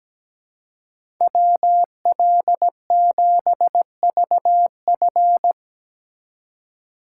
Morse code, heard as a keyed tone, about 17 wpm, 700 Hz. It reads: WL7VF
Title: USA Call Signs - 5 Characters - 4x Character Spacing - 17wpm